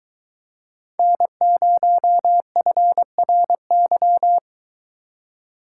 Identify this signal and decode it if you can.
Morse code, heard as a keyed tone, about 23 wpm, 700 Hz. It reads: N0FRY